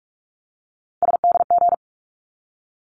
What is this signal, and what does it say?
Morse code, heard as a keyed tone, about 44 wpm, 700 Hz. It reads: SDG
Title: ARRL Sections 45wpm